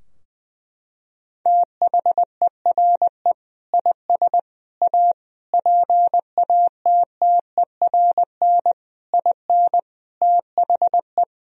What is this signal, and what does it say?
Morse code, heard as a keyed tone, about 20 wpm, 700 Hz. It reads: THERE IS A PATTERN IN THE
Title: Sentences from Top 1000 Words - Review Part 2 - 20wpm